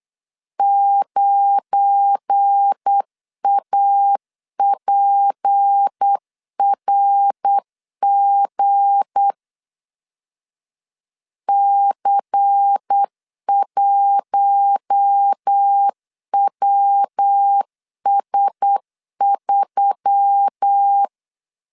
Morse code, heard as a keyed tone, about 8 wpm, 800 Hz. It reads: 9APRG C1WS3
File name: LEZIONI / 7 GF9Z / 7 GF9 8wpm 3545